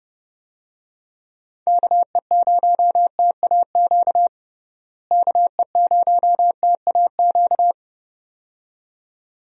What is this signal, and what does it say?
Morse code, heard as a keyed tone, about 31 wpm, 700 Hz. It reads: KE0TAQ KE0TAQ
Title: USA Call Signs - 6 Characters - Repeated 2x - 30wpm